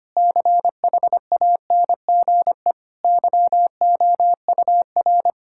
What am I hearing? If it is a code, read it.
Morse code, heard as a keyed tone, about 25 wpm, 700 Hz. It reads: CHANGE YOUR
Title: Sentences from Top 1000 Words - Review Part 6 - 25wpm